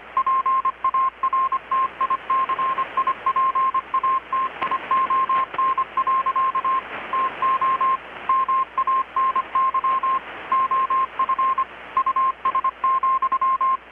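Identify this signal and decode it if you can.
Morse code, heard as a keyed tone, about 25 wpm, 1060 Hz, with radio noise. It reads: PARTICIPATION. TO MANY OF US,
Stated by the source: Morse from KPH's Night of Nights XX maritime radiotelegraphy event, received in Chiapas, Mexico about 3,800 km away on a Kenwood R-600 with a grounded 10 m long-wire antenna.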